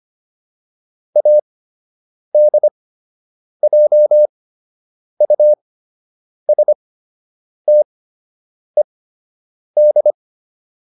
Morse code, heard as a keyed tone, about 25 wpm, spaced as for 8 wpm, 600 Hz. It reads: ADJUSTED